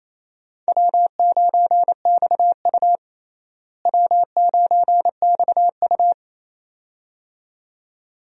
Morse code, heard as a keyed tone, about 28 wpm, 700 Hz. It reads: W9XU W9XU
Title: USA Call Signs - 4 Characters - Repeated 2x - 28wpm